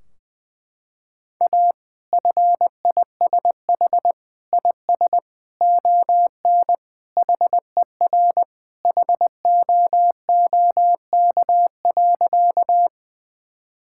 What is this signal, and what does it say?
Morse code, heard as a keyed tone, about 20 wpm, 700 Hz. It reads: A FISH IS ON HER HOOK.
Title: Sentences from Top 1600 Words 20wpm